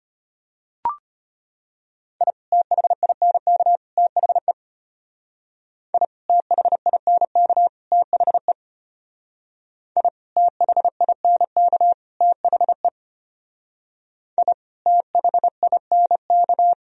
Morse code, heard as a keyed tone, about 33 wpm, 700 Hz. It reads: I THINK THE I THINK THE I THINK THE I THINK